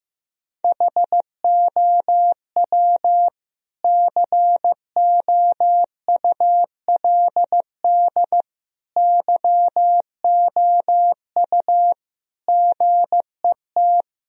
Morse code, heard as a keyed tone, about 15 wpm, 700 Hz. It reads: HOW COULD YOU GET
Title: Sentences from Top 1000 Words - Review Part 2 - 15wpm